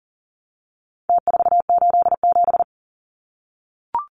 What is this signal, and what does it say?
Morse code, heard as a keyed tone, about 40 wpm, 700 Hz. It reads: T487
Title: Numbers - Contest Style - 4-Digit 40wpm